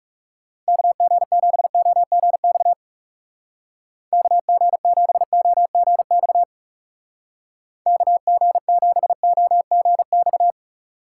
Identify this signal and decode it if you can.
Morse code, heard as a keyed tone, about 36 wpm, 700 Hz. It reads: KG7OGX KG7OGX KG7OGX
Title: Speed Racing - USA 2x3 Call Signs 30wpm